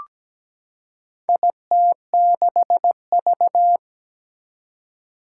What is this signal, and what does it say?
Morse code, heard as a keyed tone, about 17 wpm, 700 Hz. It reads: IT6V
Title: Four Challenging Letter-Number Combinations 17wpm